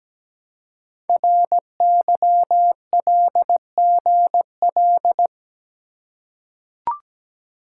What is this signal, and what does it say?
Morse code, heard as a keyed tone, about 17 wpm, 700 Hz. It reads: RYLGL
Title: Code Groups - 5 Letters - 17wpm